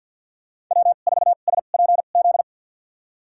Morse code, heard as a keyed tone, about 50 wpm, spaced as for 36 wpm, 700 Hz. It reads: W4SPZ
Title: USA Call Signs - 5 Characters - 2x Character Spacing - 50wpm